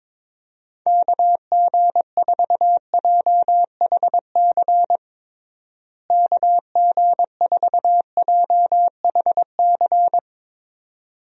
Morse code, heard as a keyed tone, about 21 wpm, 700 Hz. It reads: KG4JHC KG4JHC
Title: USA Call Signs - 6 Characters - Repeated 2x - 22wpm